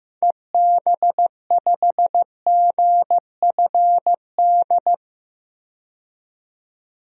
Morse code, heard as a keyed tone, about 15 wpm, 700 Hz. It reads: EB5GFD
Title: DXCC-100 Call Signs - 6 Characters - 15wpm